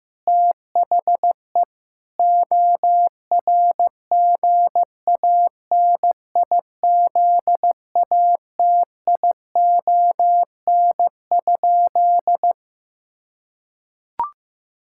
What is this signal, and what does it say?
Morse code, heard as a keyed tone, about 15 wpm, 700 Hz. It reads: THE ORGANIZATION?
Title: Sentences from Top 1000 Words - Review Part 5 - 15wpm